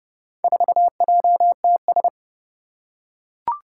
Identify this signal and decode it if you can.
Morse code, heard as a keyed tone, about 30 wpm, 700 Hz. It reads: 4JTS
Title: Four Challenging Letter-Number Combinations 30wpm